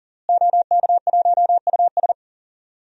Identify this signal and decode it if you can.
Morse code, heard as a keyed tone, about 40 wpm, 700 Hz. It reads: OK1US
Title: DXCC-100 Call Signs - 5 Characters - Repeated 3x - 40wpm